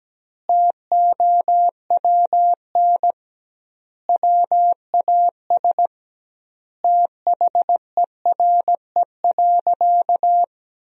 Morse code, heard as a keyed tone, about 17 wpm, 700 Hz. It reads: TOWN WAS THERE.
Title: 2x Word-Spacing - Sentences from Top 500 Words - 17wpm